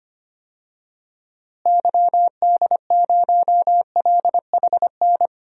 Morse code, heard as a keyed tone, about 25 wpm, 700 Hz. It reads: YD0LHN
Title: DXCC-100 Call Signs - 6 Characters - Repeated 2x - 25wpm